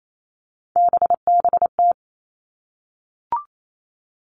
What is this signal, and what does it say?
Morse code, heard as a keyed tone, about 28 wpm, 700 Hz. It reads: BBT
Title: Three Challenging Letter-Number Combinations 28wpm